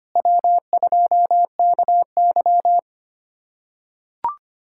Morse code, heard as a keyed tone, about 25 wpm, 700 Hz. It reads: W2KY
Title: USA Callsign DE Callsign - 4 Characters - 25wpm